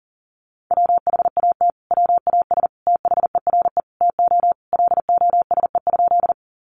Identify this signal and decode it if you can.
Morse code, heard as a keyed tone, about 40 wpm, 700 Hz. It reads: WHAT WAS THERE TO LOSE?
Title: Sentences from Top 1100 Words 40wpm